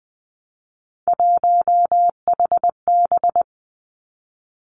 Morse code, heard as a keyed tone, about 20 wpm, 700 Hz. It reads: 1HB
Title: Three Challenging Letter-Number Combinations 20wpm